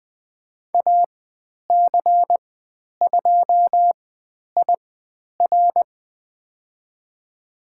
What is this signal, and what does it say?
Morse code, heard as a keyed tone, about 20 wpm, spaced as for 10 wpm, 700 Hz. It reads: AC2IR
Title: 095 Lesson 16g - Letter c - Call Signs 20wpm